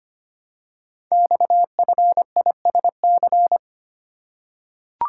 Morse code, heard as a keyed tone, about 25 wpm, 700 Hz. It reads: XFISC
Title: Code Groups - 5 Letters - 25wpm